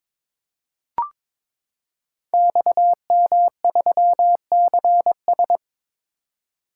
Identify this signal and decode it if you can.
Morse code, heard as a keyed tone, about 23 wpm, 700 Hz. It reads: XM3CS
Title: Code Groups - 5 Letter:Numbers - 22wpm